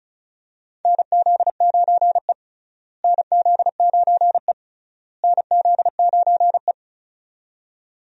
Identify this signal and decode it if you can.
Morse code, heard as a keyed tone, about 36 wpm, 700 Hz. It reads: NZ9E NZ9E NZ9E
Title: USA Call Signs - 4 Characters - Repeated 3x - 35wpm